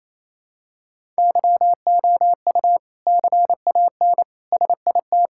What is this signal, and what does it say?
Morse code, heard as a keyed tone, about 28 wpm, 700 Hz. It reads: YOU CAN SIT
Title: Sentences from Top 1000 Words - Review Part 5 - 28wpm